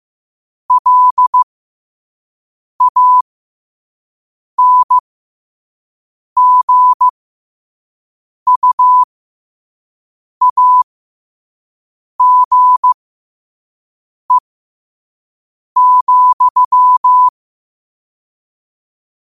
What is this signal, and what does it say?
Morse code, heard as a keyed tone, about 15 wpm, spaced as for 5 wpm, 1000 Hz. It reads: LANGUAGE,
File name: qotdfiles / 5wpm